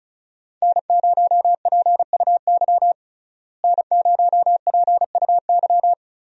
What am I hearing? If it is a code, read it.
Morse code, heard as a keyed tone, about 34 wpm, 700 Hz. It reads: N0PUY N0PUY
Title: USA Call Signs - 5 Characters - Repeated 2x - 35wpm